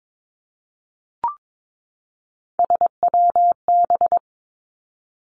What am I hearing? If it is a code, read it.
Morse code, heard as a keyed tone, about 22 wpm, 700 Hz. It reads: SWB
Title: Three Challenging Letter-Number Combinations 22wpm